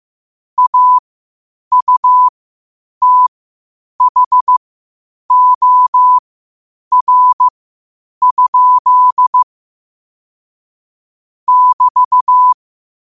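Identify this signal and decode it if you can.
Morse code, heard as a keyed tone, about 15 wpm, spaced as for 8 wpm, 1000 Hz. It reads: AUTHOR? <BT>
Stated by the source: Morse code quotes at 8 words per minute from QOTD at 8 WPM.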